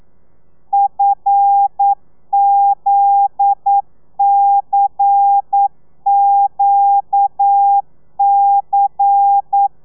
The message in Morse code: FZCQC